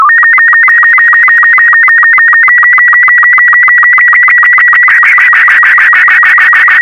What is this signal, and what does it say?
SSTV (slow-scan television), Robot 36.